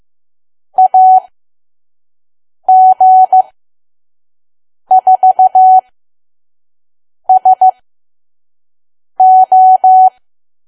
Morse code, heard as a keyed tone, about 15 wpm, spaced as for 5 wpm, 750 Hz. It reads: AG4SO
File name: Archive / 5 WPM files / PM